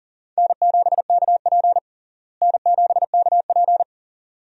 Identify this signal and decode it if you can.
Morse code, heard as a keyed tone, about 40 wpm, 700 Hz. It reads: N7KP N7KP
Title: USA Call Signs - 4 Characters - Repeated 2x - 40wpm